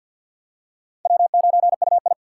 Morse code, heard as a keyed tone, about 47 wpm, 690 Hz. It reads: W9UI
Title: USA Call Signs - 4x Character Spacing - 50wpm